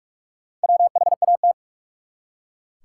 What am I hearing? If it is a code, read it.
Morse code, heard as a keyed tone, about 44 wpm, 700 Hz. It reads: WHAT
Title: Top 180 QSO Elements 45wpm